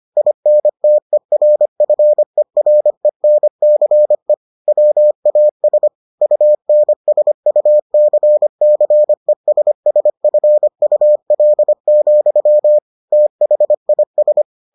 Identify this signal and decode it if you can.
Morse code, heard as a keyed tone, about 25 wpm, 600 Hz. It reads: INTERFERENCE WAS UNSUCCESSFUL, THIS